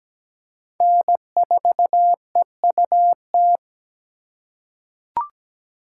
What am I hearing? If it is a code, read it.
Morse code, heard as a keyed tone, about 17 wpm, 700 Hz. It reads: N4EUT
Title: USA Call Signs - 5 Characters - 6x Character Spacing - 17wpm